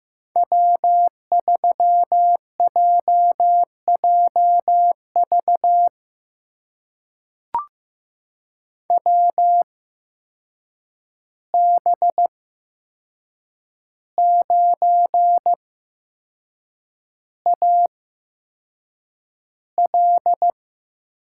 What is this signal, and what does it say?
Morse code, heard as a keyed tone, about 15 wpm, spaced as for 6 wpm, 700 Hz. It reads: W3JJV WB9AL
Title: USA Call Signs - 5 Characters - 8x Character Spacing - 15wpm